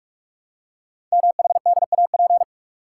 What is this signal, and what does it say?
Morse code, heard as a keyed tone, about 44 wpm, 690 Hz. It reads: MHDAP
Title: Code Groups - 5 Letters - 45wpm